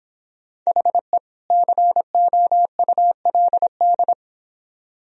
Morse code, heard as a keyed tone, about 31 wpm, 700 Hz. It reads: HE COULD